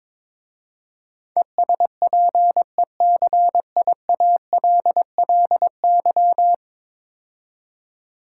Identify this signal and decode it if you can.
Morse code, heard as a keyed tone, about 22 wpm, 700 Hz. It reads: ESPECIALLY
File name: practice / 22wpm / 10 Letter Words - First 3 Letters Given - 22wpm